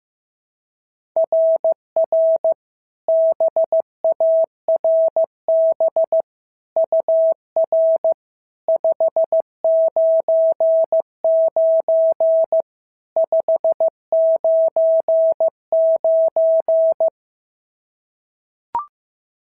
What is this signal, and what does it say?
Morse code, heard as a keyed tone, about 14 wpm, 650 Hz. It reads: RR BARB UR 599 599
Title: SOTA QSO Practice 15wpm